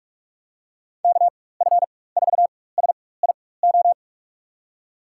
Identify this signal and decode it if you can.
Morse code, heard as a keyed tone, about 41 wpm, spaced as for 20 wpm, 700 Hz. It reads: KF4SIO
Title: USA Call Signs - 6 Characters - 4x Character Spacing - 45wpm